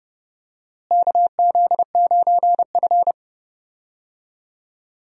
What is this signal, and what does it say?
Morse code, heard as a keyed tone, about 30 wpm, 700 Hz. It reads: KZ9F
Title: USA Call Signs - 4 Characters - Repeated 2x - 30wpm